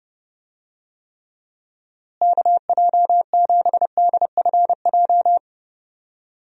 Morse code, heard as a keyed tone, about 30 wpm, 700 Hz. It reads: KJ7DFJ
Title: USA Call Signs - 6 Characters - Repeated 2x - 30wpm